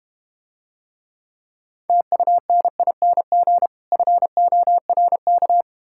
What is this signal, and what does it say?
Morse code, heard as a keyed tone, about 31 wpm, 700 Hz. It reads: TUNING FORK